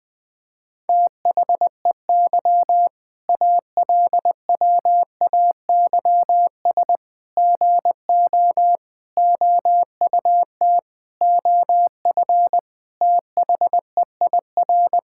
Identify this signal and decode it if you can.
Morse code, heard as a keyed tone, about 20 wpm, 700 Hz. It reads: THEY ALWAYS GO OUT OF THEIR
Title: Sentences with 6-Word Phrases - Spoken Beginning - 20wpm